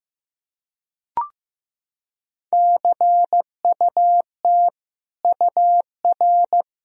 Morse code, heard as a keyed tone, about 15 wpm, 700 Hz. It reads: CUT UR
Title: Sentences from Top 300 Words and Abbreviations 15wpm